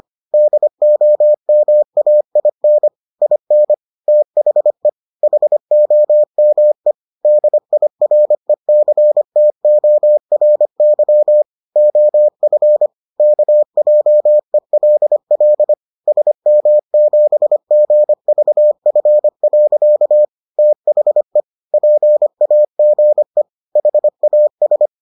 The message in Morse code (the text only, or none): DOMAIN IN THE HOME DIRECTORY OF KJELL SM7GVF. THE PAGE HAS